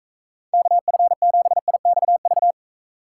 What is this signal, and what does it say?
Morse code, heard as a keyed tone, about 36 wpm, 700 Hz. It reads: KF7IXV